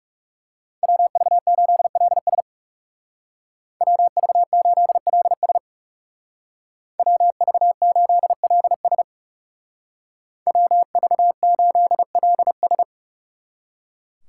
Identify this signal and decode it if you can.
Morse code, heard as a keyed tone, about 36 wpm, 700 Hz. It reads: WV8LS WV8LS WV8LS WV8LS
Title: Speed Racing - USA Call Signs 30wpm